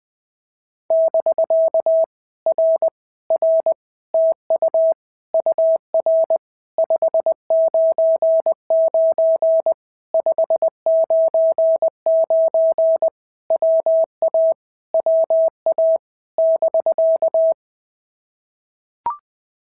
Morse code, heard as a keyed tone, about 20 wpm, 650 Hz. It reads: <BK> R R TU UR 599 599 WA WA <BK>
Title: POTA QSO Practice - 20wpm